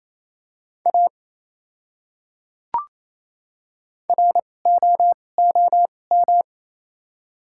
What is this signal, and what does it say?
Morse code, heard as a keyed tone, about 28 wpm, spaced as for 20 wpm, 700 Hz. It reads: A ROOM
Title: 2x Character Spacing - Top 500 Words - 28wpm